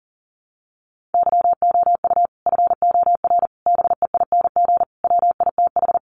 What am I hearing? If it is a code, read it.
Morse code, heard as a keyed tone, about 40 wpm, 700 Hz. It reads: YOU FOR BEING WITH
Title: Sets of 5 Words 40wpm